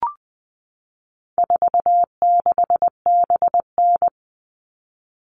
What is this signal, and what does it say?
Morse code, heard as a keyed tone, about 20 wpm, 700 Hz. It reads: 46BN
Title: All Cut Numbers - 4-Digit 20wpm